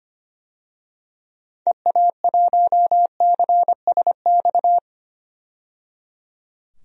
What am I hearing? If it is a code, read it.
Morse code, heard as a keyed tone, about 25 wpm, 700 Hz. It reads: EA1CSX
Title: DXCC-100 Call Signs - 6 Characters - 25wpm